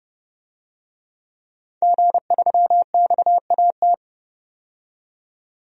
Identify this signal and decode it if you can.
Morse code, heard as a keyed tone, about 30 wpm, 700 Hz. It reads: G3XAT
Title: DXCC-100 Call Signs - 5 Characters - 30wpm